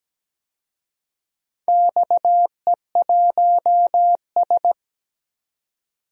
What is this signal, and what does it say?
Morse code, heard as a keyed tone, about 17 wpm, 700 Hz. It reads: XE1S
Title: DXCC-100 Call Signs - 4 Characters - 17wpm